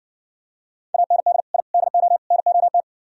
Morse code, heard as a keyed tone, about 63 wpm, 700 Hz. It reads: AND I DO NOT